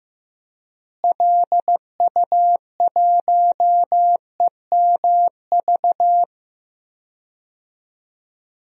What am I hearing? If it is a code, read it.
Morse code, heard as a keyed tone, about 15 wpm, 700 Hz. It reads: LU1EMV
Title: DXCC-100 Call Signs - 6 Characters - Repeated 2x - 15wpm